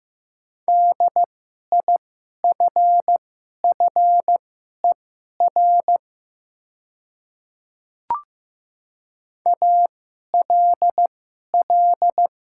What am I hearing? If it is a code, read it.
Morse code, heard as a keyed tone, about 15 wpm, spaced as for 11 wpm, 700 Hz. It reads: DIFFER ALL
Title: 2x Character Spacing - Top 200 Words - 15wpm